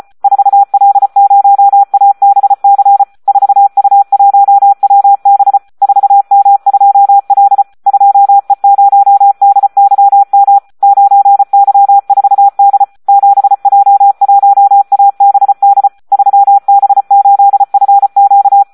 Morse code, heard as a keyed tone, about 34 wpm, 800 Hz. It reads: 4L0ABC 4U1WB 4M2L 2E0DYM 9Y4D 7J1ABD 3B8FQ
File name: Audio practica CW / SD 1 34